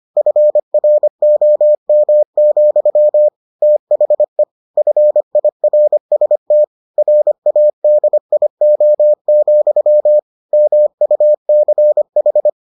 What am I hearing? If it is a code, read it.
Morse code, heard as a keyed tone, about 25 wpm, 600 Hz. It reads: FROM, THE FIRST RADIO, MUCH